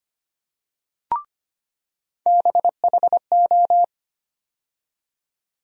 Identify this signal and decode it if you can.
Morse code, heard as a keyed tone, about 25 wpm, 700 Hz. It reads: BHO
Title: USA Call Sign Postfixes - 3 Letters 25wpm